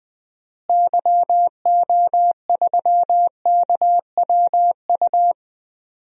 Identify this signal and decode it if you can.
Morse code, heard as a keyed tone, about 20 wpm, 700 Hz. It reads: YO3KWU
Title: DXCC-100 Call Signs - 6 Characters - Repeated 2x - 20wpm